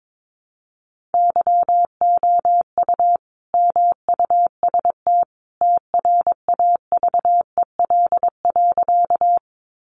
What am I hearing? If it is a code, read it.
Morse code, heard as a keyed tone, about 22 wpm, 700 Hz. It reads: YOU MUST TRAVEL.
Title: Sentences from Top 400 Words 22wpm